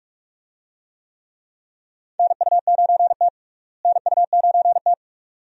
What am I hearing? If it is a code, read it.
Morse code, heard as a keyed tone, about 41 wpm, 700 Hz. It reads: NU9T NU9T
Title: USA Call Signs - 4 Characters - Repeated 2x - 45wpm